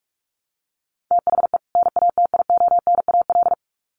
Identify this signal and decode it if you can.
Morse code, heard as a keyed tone, about 44 wpm, 700 Hz. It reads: THE NATIONAL
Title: Sets of 2 Words 45wpm